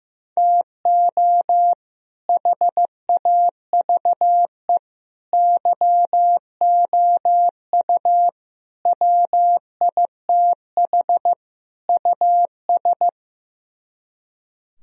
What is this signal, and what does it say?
Morse code, heard as a keyed tone, about 15 wpm, 700 Hz. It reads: TO HAVE YOU WITH US
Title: Speed-Racing - Sets of 5 Words 15wpm